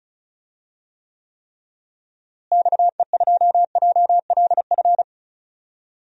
Morse code, heard as a keyed tone, about 34 wpm, 700 Hz. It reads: XE2JLF
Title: DXCC-100 Call Signs - 6 Characters - Repeated 3x - 35wpm